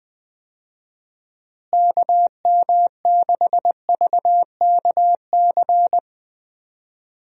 Morse code, heard as a keyed tone, about 20 wpm, 700 Hz. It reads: KM6VKC
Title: USA Call Signs - 6 Characters - Repeated 2x - 20wpm